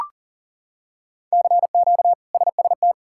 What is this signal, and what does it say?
Morse code, heard as a keyed tone, about 40 wpm, 700 Hz. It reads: CQ SST
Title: K1USN SST QSO Practice - 40wpm